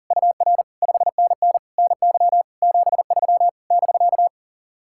Morse code, heard as a keyed tone, about 40 wpm, 700 Hz. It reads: UR 5NN NY 73 <BK>
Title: Speed-Racing Common QSO Phrases - 40wpm